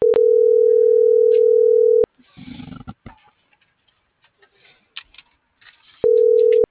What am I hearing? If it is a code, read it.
Ringback tone (North America), 440 + 480 Hz.